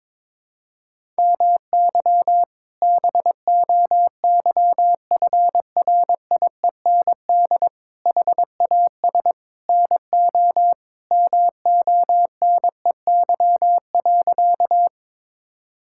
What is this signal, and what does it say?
Morse code, heard as a keyed tone, about 22 wpm, 700 Hz. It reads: MY BOYFRIEND HAS NO MONEY.
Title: Sentences from Top 1000 Words - Review Part 5 - 22wpm